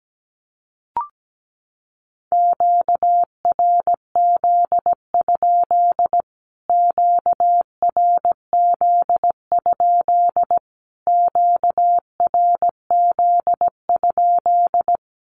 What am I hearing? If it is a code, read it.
Morse code, heard as a keyed tone, about 17 wpm, 700 Hz. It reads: QRZ? QRZ? QRZ?
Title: Q-Codes Repeated 3x 17wpm